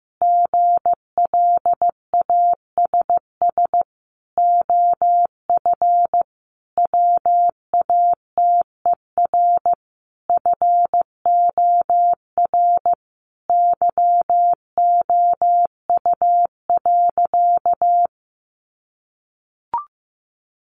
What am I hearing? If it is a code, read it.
Morse code, heard as a keyed tone, about 15 wpm, 700 Hz. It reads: GLASS OF WATER FOR YOU.
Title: Sentences from Top 1300 Words 15wpm